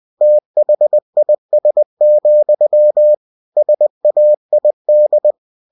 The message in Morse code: THIS, SAID

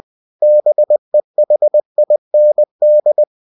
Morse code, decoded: BEHIND

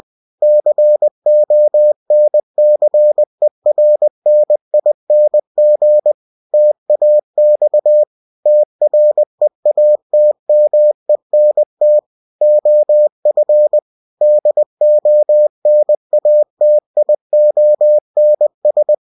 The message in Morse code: CONCERNING TAX TREATMENT OF DONATIONS

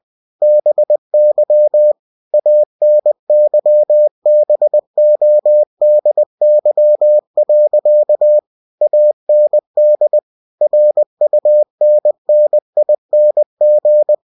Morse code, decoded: BY ANYBODY. AND RUNNING

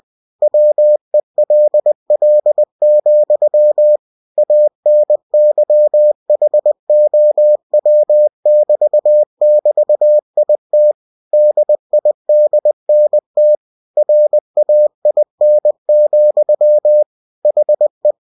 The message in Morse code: WELL, ANYHOW <BT> <BT> IT DIDNT RAIN, HE

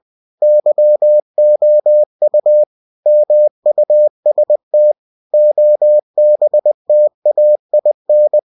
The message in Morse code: YOU MUST OBTAIN

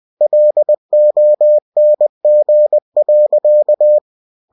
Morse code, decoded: LONG.

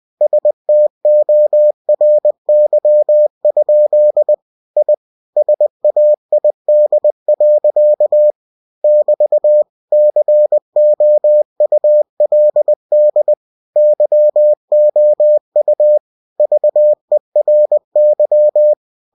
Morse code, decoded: STORY? I SAID. <BT> COULD YOU VERY

